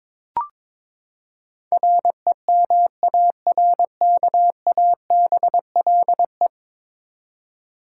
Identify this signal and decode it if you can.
Morse code, heard as a keyed tone, about 22 wpm, 700 Hz. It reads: REMARKABLE
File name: practice / 22wpm / Ten Letter Words - 22wpm